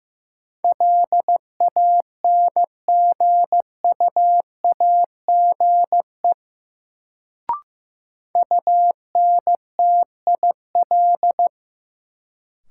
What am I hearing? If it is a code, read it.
Morse code, heard as a keyed tone, about 15 wpm, 700 Hz. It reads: LANGUAGE UNTIL
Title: Sentences from Top 500 Words - New Word Drill - 15wpm